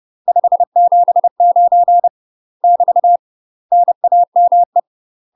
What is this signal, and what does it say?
Morse code, heard as a keyed tone, about 30 wpm, 700 Hz. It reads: 579 <BT> NAME
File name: UK MASTER QSOs TRAINING / 30WPM / UK MASTER QSOs 0031